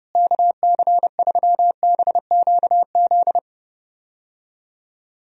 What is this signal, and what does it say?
Morse code, heard as a keyed tone, about 30 wpm, 700 Hz. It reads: KC3BQZ
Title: USA Call Signs - 6 Characters - Repeated 2x - 30wpm